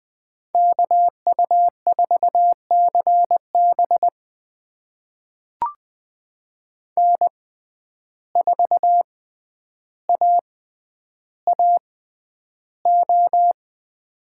Morse code, heard as a keyed tone, about 20 wpm, spaced as for 10 wpm, 700 Hz. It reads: KU4CB N4AAO
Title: USA Call Signs - 5 Characters - 6x Character Spacing - 20wpm